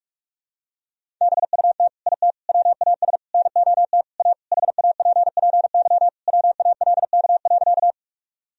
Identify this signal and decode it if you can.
Morse code, heard as a keyed tone, about 41 wpm, 700 Hz. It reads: BUT IT WAS NOT A HAPPY WALK.
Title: Sentences from Top 1000 Words - Review Part 1 - 45wpm